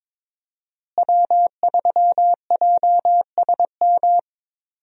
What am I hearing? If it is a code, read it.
Morse code, heard as a keyed tone, about 22 wpm, 700 Hz. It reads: W3JSM
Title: USA Call Signs - 5 Characters - Repeated 3x - 22wpm